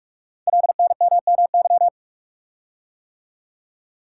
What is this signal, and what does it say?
Morse code, heard as a keyed tone, about 44 wpm, 700 Hz. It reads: LNMMY
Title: Code Groups - 5 Letters - 45wpm